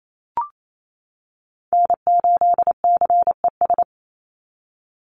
Morse code, heard as a keyed tone, about 28 wpm, 700 Hz. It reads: N8CES
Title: USA Call Signs 28wpm